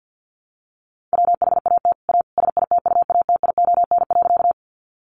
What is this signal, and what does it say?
Morse code, heard as a keyed tone, about 50 wpm, 700 Hz. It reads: WHAT A SITUATION.
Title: Sentences from Top 1200 Words 50wpm